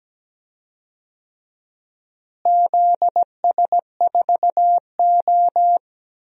Morse code, heard as a keyed tone, about 17 wpm, 700 Hz. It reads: ZS4O
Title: DXCC-100 Call Signs - 4 Characters - Repeated 3x - 17wpm